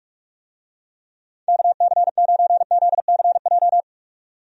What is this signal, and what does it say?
Morse code, heard as a keyed tone, about 42 wpm, 700 Hz. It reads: KC9ZCJ
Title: USA Call Signs - 6 Characters - 45wpm